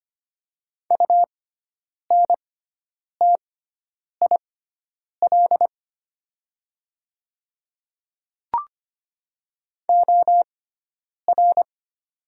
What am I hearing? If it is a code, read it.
Morse code, heard as a keyed tone, about 25 wpm, spaced as for 9 wpm, 700 Hz. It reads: UNTIL OR
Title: 6x Character Spacing - Top 500 Words - 25wpm